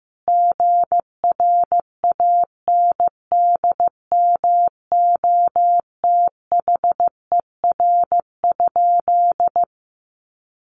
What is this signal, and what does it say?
Morse code, heard as a keyed tone, about 15 wpm, 700 Hz. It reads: GRANDMOTHER?
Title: Sentences from Top 1200 Words 15wpm